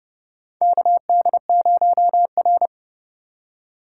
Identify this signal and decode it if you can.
Morse code, heard as a keyed tone, about 32 wpm, 700 Hz. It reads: KD0R
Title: USA Call Signs - 4 Characters - 4x Character Spacing - 30wpm